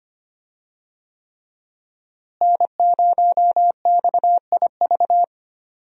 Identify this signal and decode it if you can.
Morse code, heard as a keyed tone, about 25 wpm, 700 Hz. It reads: N0XIV
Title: USA Call Signs - 5 Characters - Repeated 3x - 25wpm